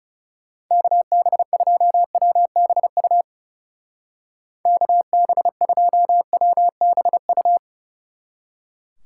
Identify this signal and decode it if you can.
Morse code, heard as a keyed tone, about 36 wpm, 700 Hz. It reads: KB2WBU KB2WBU
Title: Speed Racing - USA Call Signs 30wpm